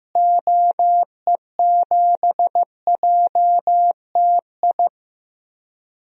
Morse code, heard as a keyed tone, about 15 wpm, 700 Hz. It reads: OE7JTI
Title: DXCC-100 Call Signs - 6 Characters - Repeated 3x - 15wpm